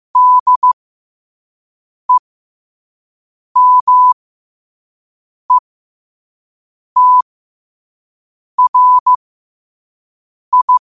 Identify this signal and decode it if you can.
Morse code, heard as a keyed tone, about 15 wpm, spaced as for 5 wpm, 1000 Hz. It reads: DEMETRI
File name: qotdfiles / 5wpm